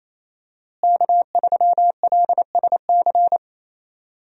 Morse code, heard as a keyed tone, about 28 wpm, 700 Hz. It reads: K3LSC
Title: USA Call Signs - 5 Characters - Repeated 3x - 28wpm